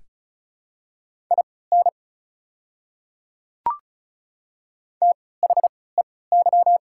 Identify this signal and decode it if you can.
Morse code, heard as a keyed tone, about 36 wpm, spaced as for 20 wpm, 690 Hz. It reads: IN THEY
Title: 3x Character Spacing - Top 100 Words - 35wpm